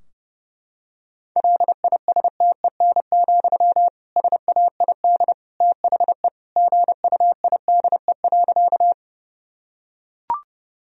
Morse code, heard as a keyed tone, about 30 wpm, 700 Hz. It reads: LISTEN, SAID THE GUIDE.
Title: Sentences from Top 1000 Words - Review Part 3 - 30wpm